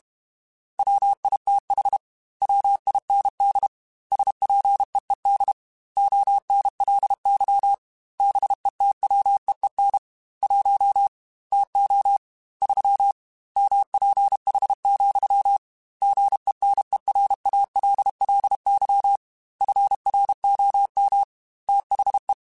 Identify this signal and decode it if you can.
Morse code, heard as a keyed tone, about 32 wpm, 800 Hz. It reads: WITH WIND SPEED ONLY BETWEEN 1 TO 3 MPH, GENERALLY FROM THE
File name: Weather Sets / Weather Sets-mp3s / Weather 5 1 @30